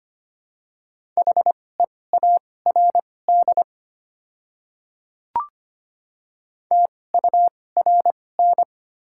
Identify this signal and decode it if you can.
Morse code, heard as a keyed tone, about 25 wpm, spaced as for 18 wpm, 700 Hz. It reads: HEARD TURN
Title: 2x Character Spacing - Top 500 Words - 25wpm